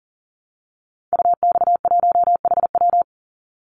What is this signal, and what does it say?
Morse code, heard as a keyed tone, about 40 wpm, 700 Hz. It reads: UX1HW
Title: Call Signs Repeated 3x - 40wpm